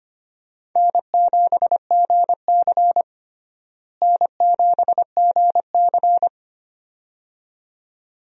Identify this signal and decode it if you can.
Morse code, heard as a keyed tone, about 25 wpm, 700 Hz. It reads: N7GC N7GC
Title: USA Call Signs - 4 Characters - Repeated 2x - 25wpm